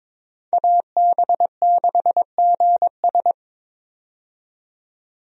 Morse code, heard as a keyed tone, about 22 wpm, 700 Hz. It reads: AB6GS
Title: USA Call Signs - 5 Characters - 22wpm